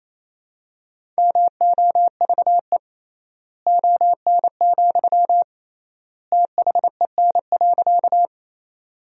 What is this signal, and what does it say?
Morse code, heard as a keyed tone, about 28 wpm, 700 Hz. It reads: MOVE ON, THEN.
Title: 3x Word-Spacing - Sentences from Top 200 Words - 28wpm